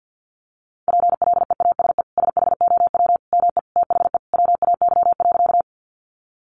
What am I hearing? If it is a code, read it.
Morse code, heard as a keyed tone, about 50 wpm, 700 Hz. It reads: PLEASE SHOW ME THE WAY.
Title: Sentences from Top 1500 Words 50wpm